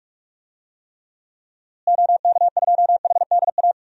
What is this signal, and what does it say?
Morse code, heard as a keyed tone, about 46 wpm, 700 Hz. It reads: OK2HDA
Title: DXCC-100 Call Signs - 6 Characters - Repeated 3x - 45wpm